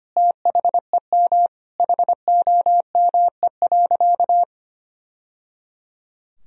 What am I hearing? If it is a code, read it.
Morse code, heard as a keyed tone, about 25 wpm, 700 Hz. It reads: THEM HOME.
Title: Sentences from Top 1000 Words - Review Part 3 - 25wpm